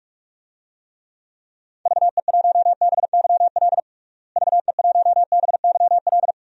Morse code, heard as a keyed tone, about 46 wpm, 700 Hz. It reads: VE1BYL VE1BYL
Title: DXCC-100 Call Signs - 6 Characters - Repeated 3x - 45wpm